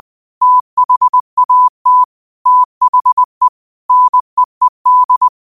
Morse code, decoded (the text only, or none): THAT THE NEED